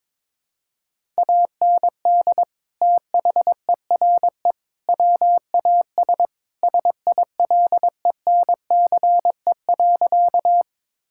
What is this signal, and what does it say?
Morse code, heard as a keyed tone, about 22 wpm, 700 Hz. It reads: AND THERE WAS SILENCE.
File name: practice / 22wpm / Sentences with 3-Word Phrases - 22wpm